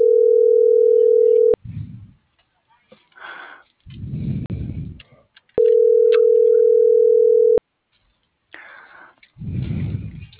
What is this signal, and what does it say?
Ringback tone (North America), 440 + 480 Hz.